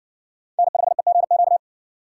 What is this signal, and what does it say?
Morse code, heard as a keyed tone, about 56 wpm, 700 Hz. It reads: N5EGQ